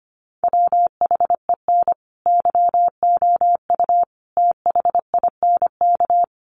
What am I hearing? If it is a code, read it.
Morse code, heard as a keyed tone, about 25 wpm, 700 Hz. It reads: WHEN YOU THINK
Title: Sets of 5 Words Encore2 25wpm